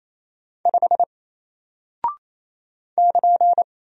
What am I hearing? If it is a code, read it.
Morse code, heard as a keyed tone, about 27 wpm, 700 Hz. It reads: 5 <KN>
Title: Single Character (Letter:Number:Punctuation:Prosign) Alt 28wpm